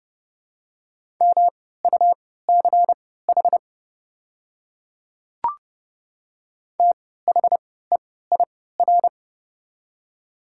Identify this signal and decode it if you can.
Morse code, heard as a keyed tone, about 30 wpm, spaced as for 17 wpm, 690 Hz. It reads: MUCH THEIR
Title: 3x Character Spacing - Top 200 Words - 30wpm